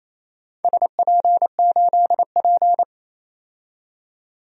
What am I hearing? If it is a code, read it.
Morse code, heard as a keyed tone, about 28 wpm, 700 Hz. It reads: SP8P
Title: DXCC-100 Call Signs - 4 Characters - 28wpm